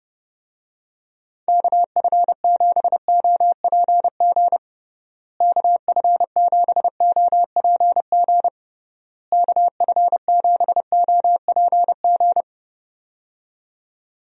Morse code, heard as a keyed tone, about 32 wpm, 700 Hz. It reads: KF7OPG KF7OPG KF7OPG
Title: USA Call Signs - 6 Characters - Repeated 3x - 30wpm